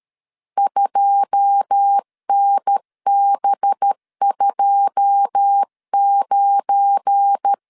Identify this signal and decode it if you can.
Morse code, heard as a keyed tone, about 13 wpm, 800 Hz. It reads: 2NB29
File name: LEZIONI / 9 2 12wpm 2301